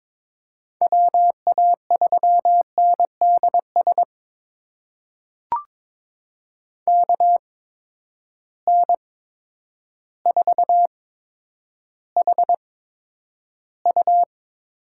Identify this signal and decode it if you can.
Morse code, heard as a keyed tone, about 21 wpm, spaced as for 10 wpm, 700 Hz. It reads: WA3NDS KN4HU
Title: USA Call Signs - 6 Characters - 8x Character Spacing - 22wpm